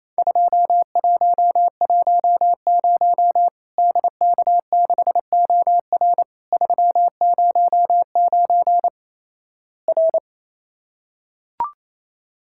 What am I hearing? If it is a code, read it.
Morse code, heard as a keyed tone, about 28 wpm, 700 Hz. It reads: 2110 DK6OR 309 R
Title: Worked All Europe DX Contest - QTC Exchange - 28wpm